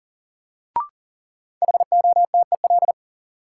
Morse code, heard as a keyed tone, about 40 wpm, 700 Hz. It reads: HOTEL
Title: Sentences from Top 1600 Words - New Word Drill - 40wpm